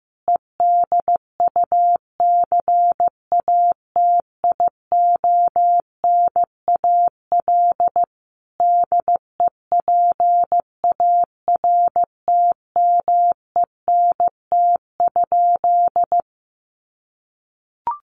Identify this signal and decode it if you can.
Morse code, heard as a keyed tone, about 15 wpm, 700 Hz. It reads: EDUCATIONAL DEPARTMENT?
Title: Sentences from Top 1200 Words 15wpm